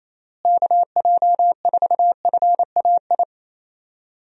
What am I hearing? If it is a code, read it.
Morse code, heard as a keyed tone, about 27 wpm, 700 Hz. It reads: KJ4FAI
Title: USA Call Signs - 6 Characters - Repeated 3x - 28wpm